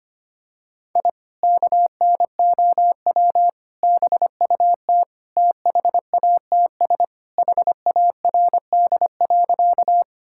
Morse code, heard as a keyed tone, about 25 wpm, 700 Hz. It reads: I KNOW BUT THATS HARD.
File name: practice / 25wpm / Sentences with 2-Word Phrases - 25wpm